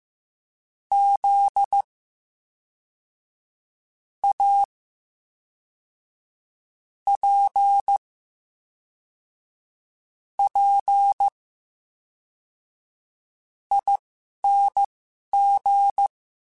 Morse code, heard as a keyed tone, about 15 wpm, spaced as for 4 wpm, 800 Hz. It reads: ZAPPING
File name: Special INGs / Special INGs-mp3s / VerySlowToFaster Ing @10